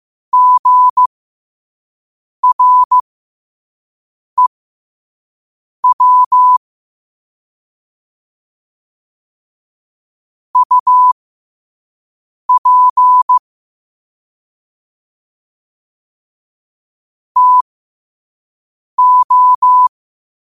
Morse code, heard as a keyed tone, about 15 wpm, spaced as for 5 wpm, 1000 Hz. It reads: GREW UP TO
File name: qotdfiles / 5wpm